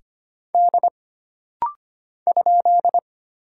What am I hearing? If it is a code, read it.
Morse code, heard as a keyed tone, about 25 wpm, 700 Hz. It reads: D ?
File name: Single Character (Letter:Number:Punctuation:Prosign) - Rapid-Fire - 25wpm